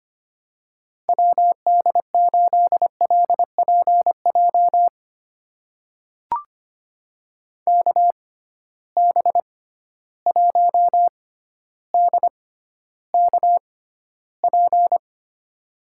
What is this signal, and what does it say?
Morse code, heard as a keyed tone, about 25 wpm, spaced as for 13 wpm, 700 Hz. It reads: WD8LPJ KB1DKP
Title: USA Call Signs - 6 Characters - 6x Character Spacing - 25wpm